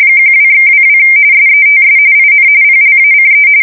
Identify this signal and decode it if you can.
RTTY, about 45.45 baud. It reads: 12/22 SETS IN PLACE